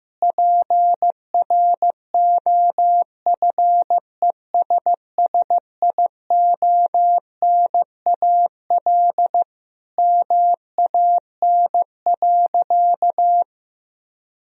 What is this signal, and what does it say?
Morse code, heard as a keyed tone, about 15 wpm, 700 Hz. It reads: PROFESSIONAL MAN.
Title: Sentences from Top 1000 Words - Review Part 4 - 15wpm